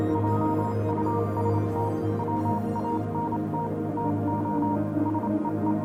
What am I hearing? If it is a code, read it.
Morse code, heard as a keyed tone, about 27 wpm, 940 Hz, with noise. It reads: JUST WANT TO SEE
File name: do you remember the song she sang (morse code version)